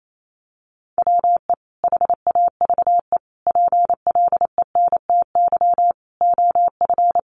Morse code, heard as a keyed tone, about 28 wpm, 700 Hz. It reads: WE HAVE PLENTY OF
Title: Sentences from Top 1300 Words 28wpm